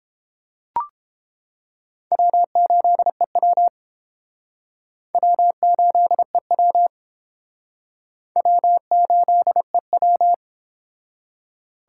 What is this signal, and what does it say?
Morse code, heard as a keyed tone, about 31 wpm, 700 Hz. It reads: W8EW W8EW W8EW